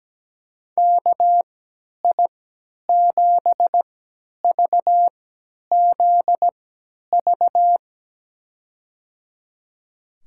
Morse code, heard as a keyed tone, about 17 wpm, spaced as for 10 wpm, 700 Hz. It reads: KI7VZV
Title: USA Call Signs - 6 Characters - 3x Character Spacing - 17wpm